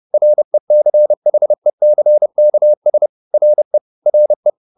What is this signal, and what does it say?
Morse code, heard as a keyed tone, about 30 wpm, 600 Hz. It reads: RECHECKS RE RE